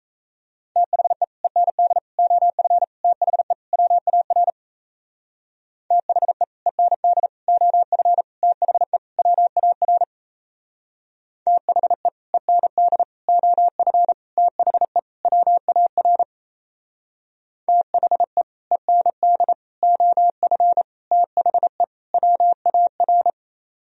Morse code, heard as a keyed tone, about 36 wpm, 700 Hz. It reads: THE END OF THE WAR THE END OF THE WAR THE END OF THE WAR THE END OF THE WAR